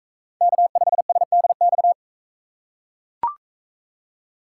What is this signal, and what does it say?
Morse code, heard as a keyed tone, about 40 wpm, 700 Hz. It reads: K5SDX